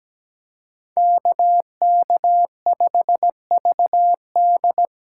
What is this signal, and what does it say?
Morse code, heard as a keyed tone, about 20 wpm, 700 Hz. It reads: KK5VD
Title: Speed Racing - USA 2x2 Call Signs 17wpm